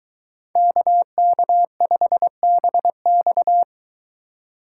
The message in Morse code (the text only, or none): KK5BX